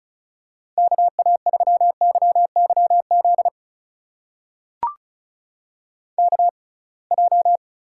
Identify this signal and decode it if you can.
Morse code, heard as a keyed tone, about 36 wpm, spaced as for 27 wpm, 700 Hz. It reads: KA3YYZ KJ
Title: USA Call Signs - 6 Characters - 6x Character Spacing - 35wpm